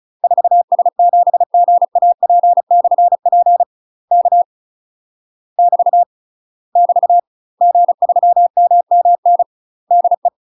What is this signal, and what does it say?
Morse code, heard as a keyed tone, about 36 wpm, 700 Hz. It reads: 4S7GAP/P K <BT> <BT> G3MMN DE